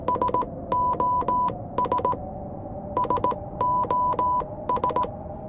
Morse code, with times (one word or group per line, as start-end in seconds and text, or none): 0.1-2.1 SOS
3.0-5.0 SOS